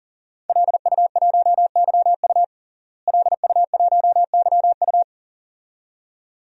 Morse code, heard as a keyed tone, about 40 wpm, 700 Hz. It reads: LU1YU LU1YU
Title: DXCC-100 Call Signs - 5 Characters - Repeated 2x - 40wpm